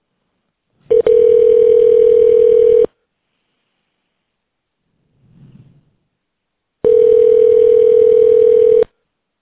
Ringback tone (North America), 440 + 480 Hz.